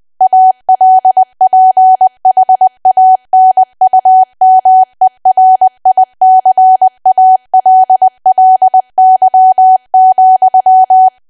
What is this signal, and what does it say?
Morse code, heard as a keyed tone, about 20 wpm, 750 Hz. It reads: ALPHANUMERICALLY,
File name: Archive / 20 WPM / 20WPM